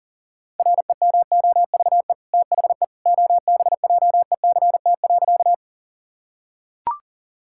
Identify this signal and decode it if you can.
Morse code, heard as a keyed tone, about 40 wpm, 700 Hz. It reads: REMOVE THE OBJECT.
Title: Sentences from Top 1000 Words - Review Part 5 - 40wpm